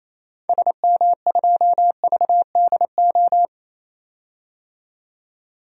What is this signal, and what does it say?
Morse code, heard as a keyed tone, about 28 wpm, 700 Hz. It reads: SM2VDO
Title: DXCC-100 Call Signs - 6 Characters - 28wpm